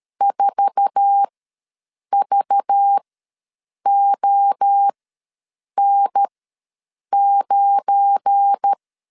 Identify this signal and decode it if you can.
Morse code, heard as a keyed tone, about 13 wpm, spaced as for 7 wpm, 800 Hz. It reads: 4VON9
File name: LEZIONI / 9 28 ws 12wpm 3354